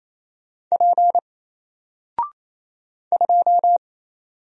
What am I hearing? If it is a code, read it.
Morse code, heard as a keyed tone, about 27 wpm, 700 Hz. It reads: P 2
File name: Single-Letter-Number-28wpm